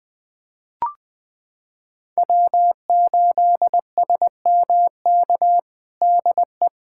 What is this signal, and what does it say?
Morse code, heard as a keyed tone, about 21 wpm, 700 Hz. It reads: W8SMK DE
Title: USA Callsign DE Callsign - 5 Characters - 20wpm